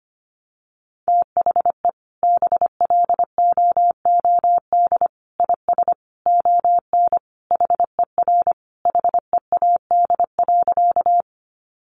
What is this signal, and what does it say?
Morse code, heard as a keyed tone, about 25 wpm, 700 Hz. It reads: THE BLOOD IS ON HER HEAD.
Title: Sentences from Top 1100 Words 25wpm